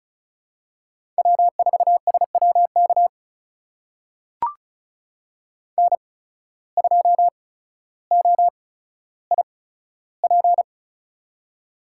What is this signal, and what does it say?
Morse code, heard as a keyed tone, about 36 wpm, spaced as for 15 wpm, 700 Hz. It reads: W4SWK N2OIP
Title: USA Call Signs - 5 Characters - 8x Character Spacing - 35wpm